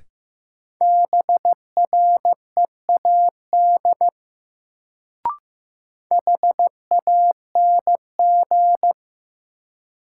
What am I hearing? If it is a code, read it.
Morse code, heard as a keyed tone, about 15 wpm, 700 Hz. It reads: BREAD HANG
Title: Sentences from Top 1600 Words - New Word Drill Alt - 15wpm